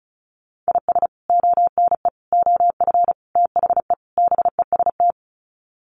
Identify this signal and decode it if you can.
Morse code, heard as a keyed tone, about 34 wpm, 700 Hz. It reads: IS ONE OF THE BEST
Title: Sets of 5 Words 35wpm